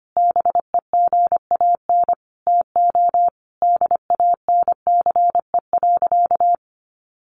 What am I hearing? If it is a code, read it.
Morse code, heard as a keyed tone, about 25 wpm, 700 Hz. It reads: BEGAN TO DANCE.
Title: Sentences from Top 900 Words 25wpm